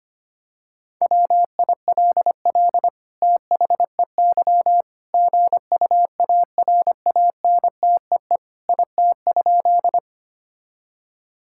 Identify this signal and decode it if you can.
Morse code, heard as a keyed tone, about 25 wpm, 700 Hz. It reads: WILL THEY GUARANTEE IT?
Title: Sentences from Top 1600 Words 25wpm